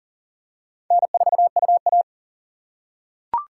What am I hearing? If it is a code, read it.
Morse code, heard as a keyed tone, about 40 wpm, 700 Hz. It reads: N4UA
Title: USA Call Signs - 4 Characters - 2x Character Spacing - 40wpm